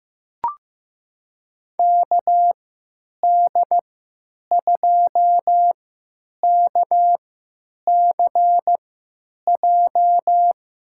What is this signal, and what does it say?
Morse code, heard as a keyed tone, about 15 wpm, spaced as for 9 wpm, 700 Hz. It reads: KD2KCJ
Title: USA Call Signs - 6 Characters - 3x Character Spacing - 15wpm